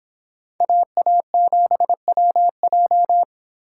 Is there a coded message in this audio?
Yes: Morse code from 0.6 to 3.2 s, heard as a keyed tone, about 25 wpm, 700 Hz. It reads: AA7WJ